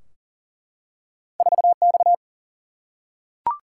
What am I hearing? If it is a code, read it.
Morse code, heard as a keyed tone, about 40 wpm, 700 Hz. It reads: 4X
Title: Single-Letter Single-Number 40wpm